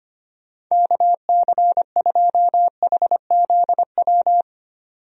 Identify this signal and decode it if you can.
Morse code, heard as a keyed tone, about 25 wpm, 700 Hz. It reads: KC2HZW
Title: USA Call Signs - 6 Characters - 25wpm